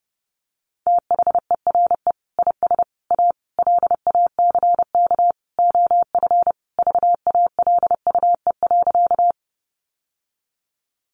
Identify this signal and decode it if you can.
Morse code, heard as a keyed tone, about 30 wpm, 700 Hz. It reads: THERE IS A LACK OF VALUE.
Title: Sentences from Top 1100 Words 30wpm